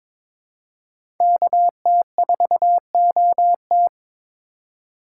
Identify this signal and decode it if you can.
Morse code, heard as a keyed tone, about 22 wpm, 700 Hz. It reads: KT4OT
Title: USA Call Signs - 5 Characters - 2x Character Spacing - 22wpm